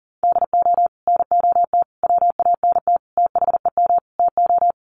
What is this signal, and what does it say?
Morse code, heard as a keyed tone, about 40 wpm, 700 Hz. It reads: DO NOT WANT THEM TO
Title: Sets of 5 Words Encore 40wpm